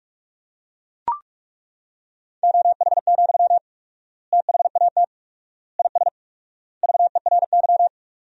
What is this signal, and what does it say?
Morse code, heard as a keyed tone, about 41 wpm, 690 Hz. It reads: OH, THAT IS VERY
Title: 4x Word-Spacing - Sentences from Top 500 Words - 45wpm